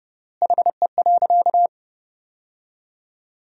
Morse code, heard as a keyed tone, about 30 wpm, 700 Hz. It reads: HE.
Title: 3x Word-Spacing - Sentences from Top 100 Words - 30wpm